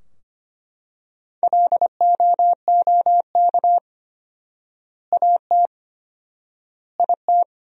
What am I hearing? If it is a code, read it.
Morse code, heard as a keyed tone, about 25 wpm, 700 Hz. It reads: LOOK AT IT
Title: 4x Word-Spacing - Sentences from Top 200 Words - 25wpm